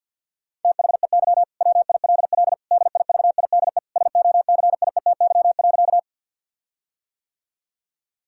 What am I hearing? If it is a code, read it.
Morse code, heard as a keyed tone, about 50 wpm, 700 Hz. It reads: THEY WILL DIVIDE SOCIETY.
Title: Sentences from Top 1600 Words 50wpm